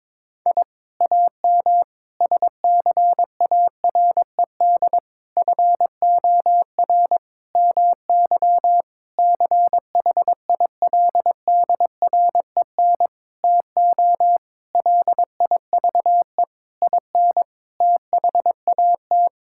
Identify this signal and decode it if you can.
Morse code, heard as a keyed tone, about 21 wpm, 700 Hz. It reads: I AM SCARED FOR MY CHILDREN TO LIVE IN THAT
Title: Sentences from Top 1000 Words - Review Part 5 - 22wpm